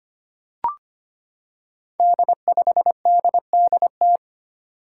Morse code, heard as a keyed tone, about 25 wpm, 700 Hz. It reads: D5DDT
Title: Five Challenging Letter-Number Combinations 25wpm